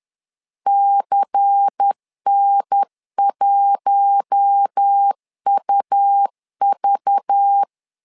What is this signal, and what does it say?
Morse code, heard as a keyed tone, about 10 wpm, 800 Hz. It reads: CN1UV
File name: LEZIONI / 5 LBVC / 5 LBVC 10wpm 2349